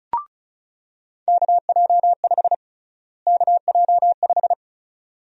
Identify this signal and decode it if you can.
Morse code, heard as a keyed tone, about 36 wpm, 700 Hz. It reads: KJ5 KJ5
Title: USA Call Sign Prefixes - Repeated 2x - 35wpm